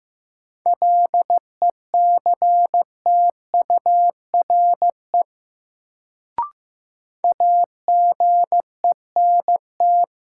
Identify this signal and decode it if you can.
Morse code, heard as a keyed tone, about 15 wpm, 700 Hz. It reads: LECTURE AGENT
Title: Sentences from Top 1600 Words - New Word Drill - 15wpm